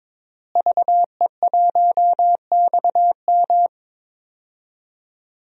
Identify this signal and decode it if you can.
Morse code, heard as a keyed tone, about 22 wpm, 700 Hz. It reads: VE1XM
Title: DXCC-100 Call Signs - 5 Characters - Repeated 2x - 22wpm